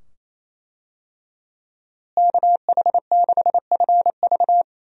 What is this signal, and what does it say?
Morse code, heard as a keyed tone, about 28 wpm, 700 Hz. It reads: KH6FV
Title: USA Call Signs - 5 Characters - Repeated 2x - 28wpm